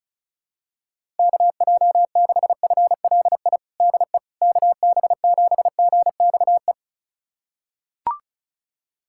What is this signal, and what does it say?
Morse code, heard as a keyed tone, about 34 wpm, 700 Hz. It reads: KJ6FLI DE KB7GXE
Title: USA Callsign DE Callsign - 6 Characters - 35wpm